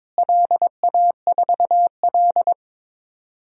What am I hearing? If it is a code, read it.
Morse code, heard as a keyed tone, about 21 wpm, 700 Hz. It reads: LA4L
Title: DXCC-100 Call Signs - 4 Characters - Repeated 3x - 22wpm